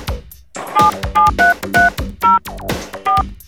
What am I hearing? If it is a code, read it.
DTMF, heard as telephone touch tones, tones of about 170 ms. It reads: 77330 4